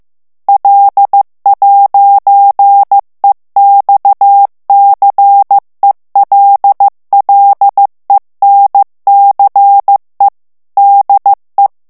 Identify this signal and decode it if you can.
Morse code, heard as a keyed tone, about 15 wpm, 800 Hz. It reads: L'EXCELLENCE DE
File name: morse mp3 / FRANCAIS TEXTE CLAIR 2 VITESSE 14 MOTS MINUTES